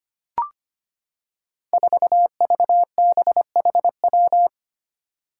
Morse code, heard as a keyed tone, about 25 wpm, 700 Hz. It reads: 4VBHW
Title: Five Challenging Letter-Number Combinations 25wpm